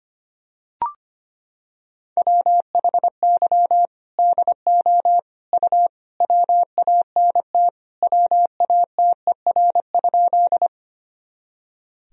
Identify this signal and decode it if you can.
Morse code, heard as a keyed tone, about 25 wpm, 700 Hz. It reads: WHY DO U WANT WATER?
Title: Sentences from Top 200 Words and Abbreviations 25wpm